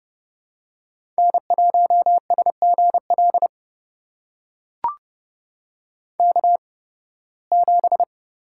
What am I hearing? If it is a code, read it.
Morse code, heard as a keyed tone, about 30 wpm, spaced as for 19 wpm, 700 Hz. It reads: N1SGL K7
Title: USA Call Signs - 5 Characters - 8x Character Spacing - 30wpm